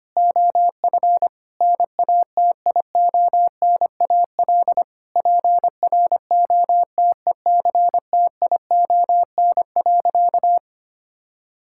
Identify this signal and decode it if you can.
Morse code, heard as a keyed tone, about 25 wpm, 700 Hz. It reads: OF NATIONAL PROTECTION.
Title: Sentences from Top 1600 Words 25wpm